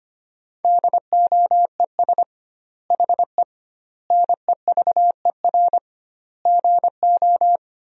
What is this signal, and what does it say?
Morse code, heard as a keyed tone, about 25 wpm, 700 Hz. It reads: DOES HE NEVER GO
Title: 2x Word-Spacing - Sentences from Top 300 Words - 25wpm